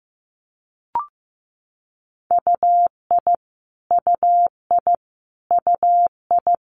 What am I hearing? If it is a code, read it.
Morse code, heard as a keyed tone, about 15 wpm, 700 Hz. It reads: UI UI UI
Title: Multi-letter Phonemes Repeated 3x 15wpm